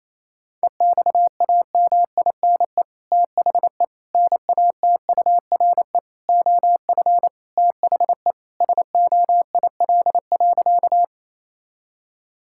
Morse code, heard as a keyed tone, about 28 wpm, 700 Hz. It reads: EXAMINE THE NATURE OF THE SOIL.
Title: Sentences from Top 1600 Words 28wpm